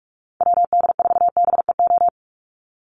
Morse code, heard as a keyed tone, about 44 wpm, 700 Hz. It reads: WD4BEO
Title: USA Call Signs 45wpm